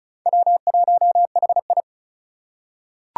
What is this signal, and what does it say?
Morse code, heard as a keyed tone, about 36 wpm, 700 Hz. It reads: W1HI
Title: Four Challenging Letter-Number Combinations 35wpm